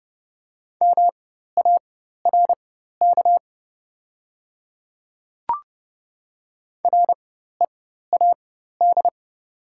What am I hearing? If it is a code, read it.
Morse code, heard as a keyed tone, about 29 wpm, spaced as for 14 wpm, 690 Hz. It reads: MARK READ
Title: 4x Character Spacing - Top 300 Words - 30wpm